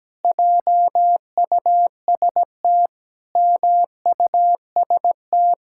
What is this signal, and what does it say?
Morse code, heard as a keyed tone, about 17 wpm, 700 Hz. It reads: JUST MUST
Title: Pangram Challenge v2 - 17wpm